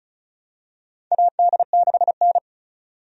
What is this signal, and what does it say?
Morse code, heard as a keyed tone, about 34 wpm, 700 Hz. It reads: AD6N
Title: USA Call Signs - 4 Characters - 35wpm